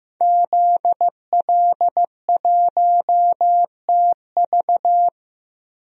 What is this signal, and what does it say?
Morse code, heard as a keyed tone, about 16 wpm, 700 Hz. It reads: ZL1TV
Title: DXCC-100 Call Signs - 5 Characters - Repeated 3x - 15wpm